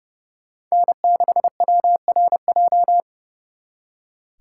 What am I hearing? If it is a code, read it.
Morse code, heard as a keyed tone, about 38 wpm, 700 Hz. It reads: N6WRJ